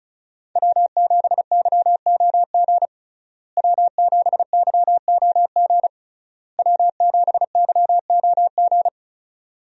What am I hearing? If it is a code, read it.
Morse code, heard as a keyed tone, about 36 wpm, 700 Hz. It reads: W7YOG W7YOG W7YOG
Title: USA Call Signs - 5 Characters - Repeated 3x - 35wpm